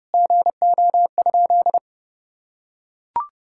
Morse code, heard as a keyed tone, about 32 wpm, 700 Hz. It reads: GO?
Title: Sentences from Top 1000 Words - Review Part 6 - 30wpm